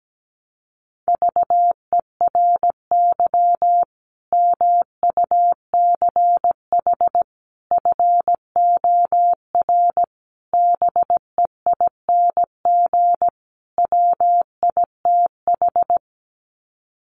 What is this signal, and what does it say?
Morse code, heard as a keyed tone, about 17 wpm, 700 Hz. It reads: VERY MUCH FOR BEING WITH
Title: Sets of 5 Words 17wpm